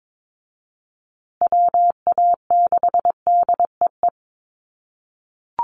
Morse code, heard as a keyed tone, about 22 wpm, 700 Hz. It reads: WA6DEE
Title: USA Call Signs Repeated 2x 22wpm